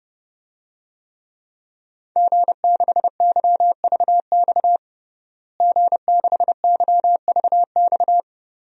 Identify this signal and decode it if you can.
Morse code, heard as a keyed tone, about 32 wpm, 700 Hz. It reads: G6YVX G6YVX
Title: DXCC-100 Call Signs - 5 Characters - Repeated 3x - 30wpm